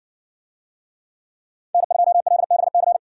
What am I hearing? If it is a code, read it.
Morse code, heard as a keyed tone, about 57 wpm, 700 Hz. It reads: N2LBC